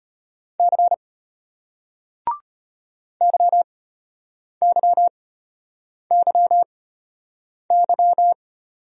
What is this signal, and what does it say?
Morse code, heard as a keyed tone, about 35 wpm, spaced as for 8 wpm, 700 Hz. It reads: C YYYY